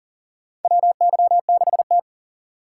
Morse code, heard as a keyed tone, about 39 wpm, 700 Hz. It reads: WY6T
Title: USA Call Signs - 4 Characters - 3x Character Spacing - 40wpm